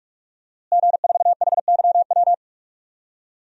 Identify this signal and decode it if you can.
Morse code, heard as a keyed tone, about 46 wpm, 700 Hz. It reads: G4HYW
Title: DXCC-100 Call Signs - 5 Characters - Repeated 3x - 45wpm